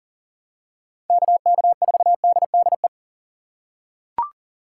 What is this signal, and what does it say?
Morse code, heard as a keyed tone, about 40 wpm, 700 Hz. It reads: KK4DDE
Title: USA Call Signs - 6 Characters - 3x Character Spacing - 40wpm